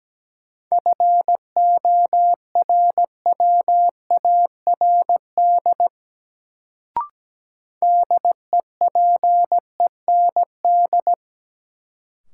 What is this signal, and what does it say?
Morse code, heard as a keyed tone, about 17 wpm, 700 Hz. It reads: FORWARD DEPEND
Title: Sentences from Top 1300 Words - New Word Drill Alt - 17wpm